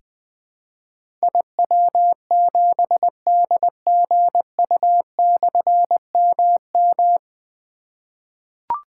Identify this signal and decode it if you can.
Morse code, heard as a keyed tone, about 20 wpm, 700 Hz. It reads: IW7DGU/MM
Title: Difficult Call Signs 20wpm